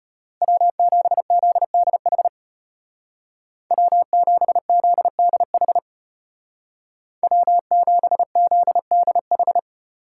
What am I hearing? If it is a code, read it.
Morse code, heard as a keyed tone, about 33 wpm, 700 Hz. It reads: W7ZDH W7ZDH W7ZDH